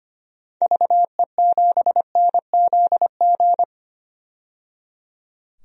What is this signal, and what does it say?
Morse code, heard as a keyed tone, about 25 wpm, 700 Hz. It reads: VE7NZG
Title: DXCC-100 Call Signs - 6 Characters - 25wpm